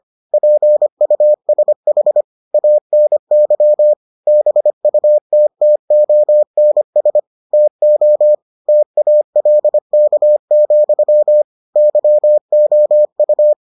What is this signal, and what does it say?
Morse code, heard as a keyed tone, about 25 wpm, 600 Hz. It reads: PUSH ANY BUTTONS TO TALK, YOU